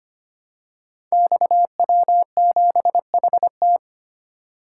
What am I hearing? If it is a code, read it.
Morse code, heard as a keyed tone, about 25 wpm, 700 Hz. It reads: XW7HT
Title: Code Groups - 5 Letter:Numbers - 25wpm